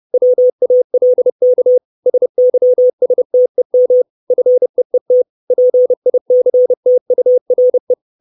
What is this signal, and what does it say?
Morse code, heard as a keyed tone, about 30 wpm, 500 Hz. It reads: WALK SYSTEM FEET PICTURE
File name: 632 Words 30WPM